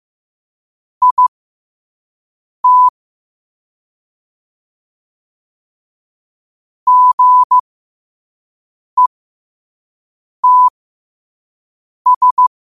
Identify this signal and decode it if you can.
Morse code, heard as a keyed tone, about 15 wpm, spaced as for 5 wpm, 1000 Hz. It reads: IT GETS
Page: Quotes for Tue, 12 Aug 2025 in Morse Code at 5 words per minute.